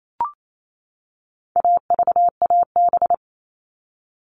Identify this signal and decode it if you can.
Morse code, heard as a keyed tone, about 28 wpm, 700 Hz. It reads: AVAB
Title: All Cut Numbers - 4-Digit 28wpm